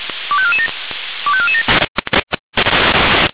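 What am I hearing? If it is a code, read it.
Five-tone selective call (ZVEI), heard as quick runs of five tones, tones of about 70 ms. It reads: 25447 25447
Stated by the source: ZVEI five-tone selective call received on an SDR.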